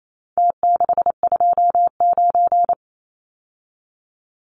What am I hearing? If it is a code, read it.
Morse code, heard as a keyed tone, about 28 wpm, 700 Hz. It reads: T629
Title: Numbers - Contest Style - 4-Digit 28wpm